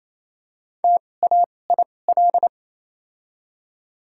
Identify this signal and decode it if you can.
Morse code, heard as a keyed tone, about 28 wpm, spaced as for 20 wpm, 700 Hz. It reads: TAIL
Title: 2x Character Spacing - Top 500 Words - 28wpm